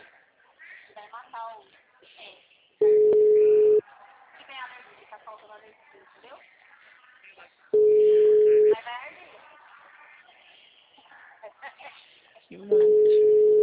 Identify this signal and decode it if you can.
Ringback tone (Europe, 425 Hz), with speech.
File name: out- -2012- - - .15086